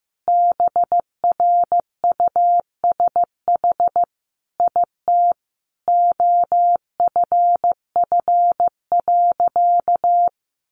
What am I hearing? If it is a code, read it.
Morse code, heard as a keyed tone, about 15 wpm, 700 Hz. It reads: BRUSH IT OFF.
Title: Sentences from Top 1100 Words 15wpm